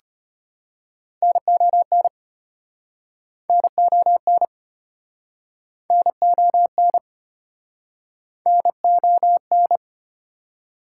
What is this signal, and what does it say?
Morse code, heard as a keyed tone, about 36 wpm, 700 Hz. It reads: NON NON NON NON